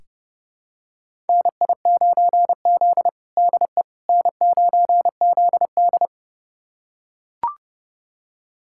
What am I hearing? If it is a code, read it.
Morse code, heard as a keyed tone, about 30 wpm, 700 Hz. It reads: NI9Z DE N9ZD
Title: USA Callsign DE Callsign - 4 Characters - 30wpm